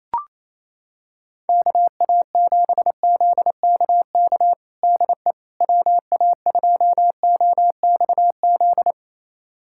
Morse code, heard as a keyed tone, about 28 wpm, 700 Hz. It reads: KA7ZKK DE WA2OXZ
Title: USA Callsign DE Callsign - 6 Characters - 28wpm